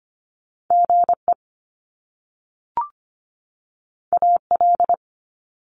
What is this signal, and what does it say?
Morse code, heard as a keyed tone, about 25 wpm, 700 Hz. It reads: GE AL
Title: Top Two Letter Combinations in Words 25wpm